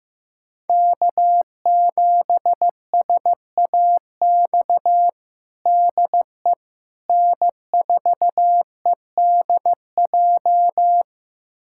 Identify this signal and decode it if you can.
Morse code, heard as a keyed tone, about 15 wpm, 700 Hz. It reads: K7SAX DE N4EDJ
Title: USA Callsign DE Callsign - 5 Characters - 15wpm